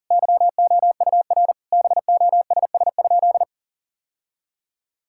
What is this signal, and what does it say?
Morse code, heard as a keyed tone, about 40 wpm, 700 Hz. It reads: YOUR BOSS?
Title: Sentences from Top 1000 Words - Review Part 6 - 40wpm